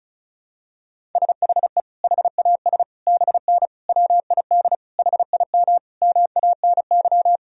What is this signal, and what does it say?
Morse code, heard as a keyed tone, about 34 wpm, 700 Hz. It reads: SHE HAS BN WID HIM MANY
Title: Sentences from Top 200 Words and Abbreviations 35wpm